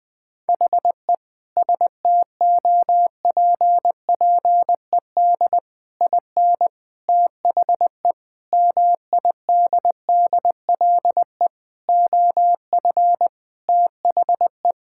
Morse code, heard as a keyed tone, about 21 wpm, 700 Hz. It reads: HE STOPPED IN THE MIDDLE OF THE
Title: Sentences with 3-Word Phrases - Spoken Beginning - 20wpm